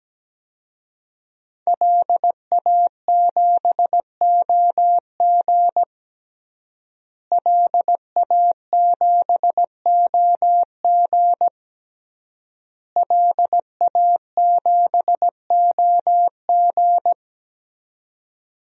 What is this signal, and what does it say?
Morse code, heard as a keyed tone, about 17 wpm, 700 Hz. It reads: LA7OG LA7OG LA7OG
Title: DXCC-100 Call Signs - 5 Characters - Repeated 3x - 17wpm